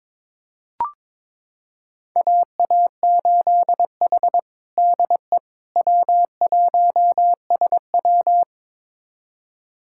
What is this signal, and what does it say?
Morse code, heard as a keyed tone, about 22 wpm, 700 Hz. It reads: AA8H DE W1SW
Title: USA Callsign DE Callsign - 4 Characters - 22wpm